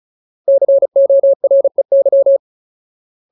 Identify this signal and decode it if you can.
Morse code, heard as a keyed tone, about 34 wpm, 550 Hz. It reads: COREY
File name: Ultimate Word List / 35WPM Ultimate